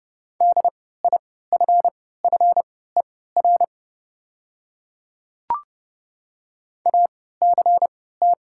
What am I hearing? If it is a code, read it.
Morse code, heard as a keyed tone, about 29 wpm, spaced as for 17 wpm, 700 Hz. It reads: DIFFER ACT
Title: 3x Character Spacing - Top 200 Words - 30wpm